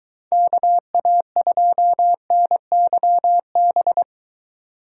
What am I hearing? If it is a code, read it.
Morse code, heard as a keyed tone, about 22 wpm, 700 Hz. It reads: KA2NYB